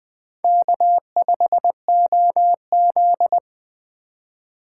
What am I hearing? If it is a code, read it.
Morse code, heard as a keyed tone, about 19 wpm, 700 Hz. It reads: K5OZ